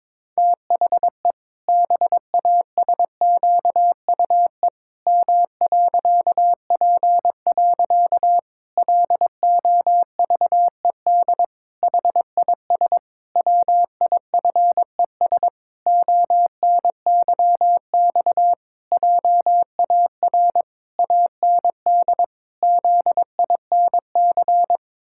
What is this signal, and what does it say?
Morse code, heard as a keyed tone, about 22 wpm, 700 Hz. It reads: THE BASQUE M.P. LOVED HIS WIFES ONYX JAR AND ZINC
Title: Pangram Challenge v2 - 22wpm